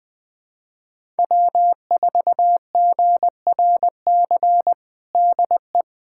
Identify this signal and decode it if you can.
Morse code, heard as a keyed tone, about 20 wpm, 700 Hz. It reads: W4GRC DE
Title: USA Callsign DE Callsign - 5 Characters - 20wpm